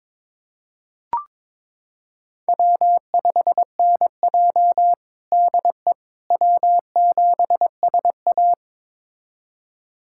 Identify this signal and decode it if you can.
Morse code, heard as a keyed tone, about 22 wpm, 700 Hz. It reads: W5NJ DE W7SA
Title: USA Callsign DE Callsign - 4 Characters - 22wpm